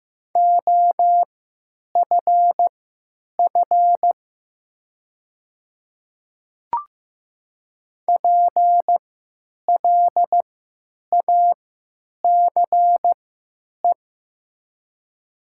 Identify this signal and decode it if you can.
Morse code, heard as a keyed tone, about 15 wpm, spaced as for 9 wpm, 700 Hz. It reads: OFF PLACE
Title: 3x Character Spacing - Top 200 Words - 15wpm